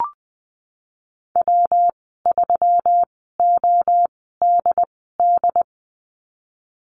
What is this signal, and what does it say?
Morse code, heard as a keyed tone, about 19 wpm, spaced as for 14 wpm, 700 Hz. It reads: W3ODD
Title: 140 Lesson 23g - Letter f - Call Signs 20wpm